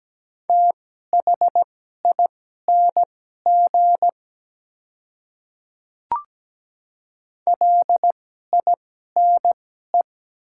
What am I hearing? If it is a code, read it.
Morse code, heard as a keyed tone, about 17 wpm, spaced as for 12 wpm, 700 Hz. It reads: THING LINE
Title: 2x Character Spacing - Top 200 Words - 17wpm